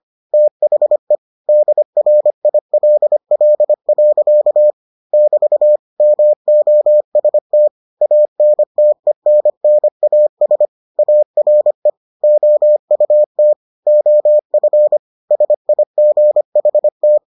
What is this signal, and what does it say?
Morse code, heard as a keyed tone, about 25 wpm, 600 Hz. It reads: THE DRILL. <BT> MOST ANTENNAS ARE OUT OF SIGHT